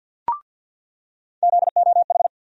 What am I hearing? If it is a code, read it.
Morse code, heard as a keyed tone, about 47 wpm, 700 Hz. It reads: ZOH
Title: USA Call Sign Postfixes - 3 Letters 50wpm